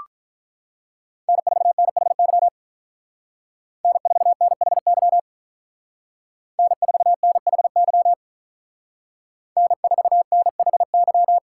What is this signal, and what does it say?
Morse code, heard as a keyed tone, about 45 wpm, 700 Hz. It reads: N4NHY N4NHY N4NHY N4NHY